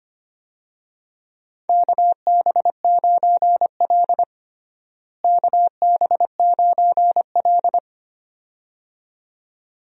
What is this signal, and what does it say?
Morse code, heard as a keyed tone, about 25 wpm, 700 Hz. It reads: KB9L KB9L
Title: USA Call Signs - 4 Characters - Repeated 2x - 25wpm